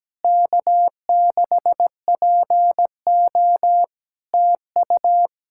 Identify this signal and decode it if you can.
Morse code, heard as a keyed tone, about 17 wpm, 700 Hz. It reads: K6PO TU
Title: Practice QSO - Exchange 4 - Ending - 17wpm